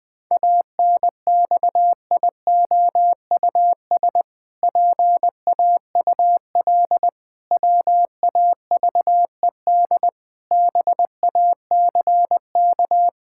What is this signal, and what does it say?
Morse code, heard as a keyed tone, about 21 wpm, 700 Hz. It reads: ANXIOUS PAUL WAVED BACK
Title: Pangram Challenge v2 - 20wpm